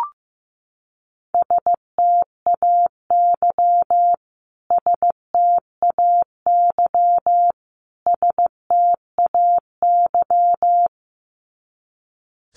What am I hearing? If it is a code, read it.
Morse code, heard as a keyed tone, about 15 wpm, 700 Hz. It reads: STAY STAY STAY
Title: Top 500 Words Repeated 15wpm